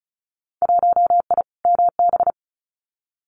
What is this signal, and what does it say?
Morse code, heard as a keyed tone, about 36 wpm, 700 Hz. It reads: 1I MB
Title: Winter Field Day Exchanges Encore 35wpm